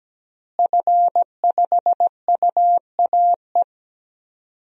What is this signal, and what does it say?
Morse code, heard as a keyed tone, about 17 wpm, 700 Hz. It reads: F5UAE
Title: DXCC-100 Call Signs - 5 Characters - Repeated 3x - 17wpm